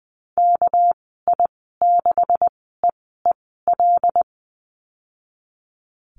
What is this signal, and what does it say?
Morse code, heard as a keyed tone, about 20 wpm, spaced as for 15 wpm, 700 Hz. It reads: KI6EEL
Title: 231 Lesson 37g - Letter x - Call Signs 20wpm